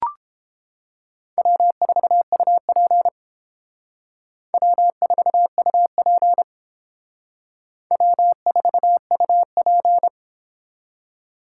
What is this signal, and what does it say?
Morse code, heard as a keyed tone, about 29 wpm, 700 Hz. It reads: W4UP W4UP W4UP